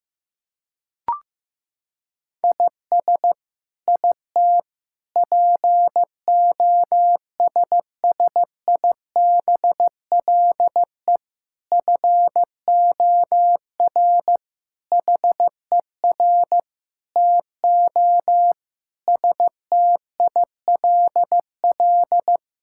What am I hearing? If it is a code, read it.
Morse code, heard as a keyed tone, about 15 wpm, 700 Hz. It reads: IS IT POSSIBLE FOR HER TO STILL
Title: Sentences from Top 1000 Words - Review Part 6 - 15wpm